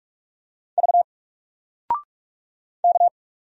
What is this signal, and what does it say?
Morse code, heard as a keyed tone, about 44 wpm, 700 Hz. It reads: V K
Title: Single Letter-Number Alt 45wpm